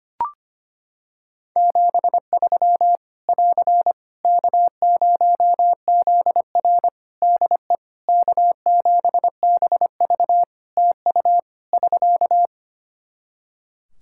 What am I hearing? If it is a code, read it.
Morse code, heard as a keyed tone, about 25 wpm, 700 Hz. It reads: 73 <AR> K0ZR DE K7BV TU <SK>
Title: K1USN SST QSO Practice 25wpm